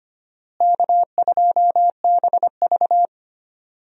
Morse code, heard as a keyed tone, about 25 wpm, 700 Hz. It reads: K2BV
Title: USA Call Signs - 4 Characters - Repeated 2x - 25wpm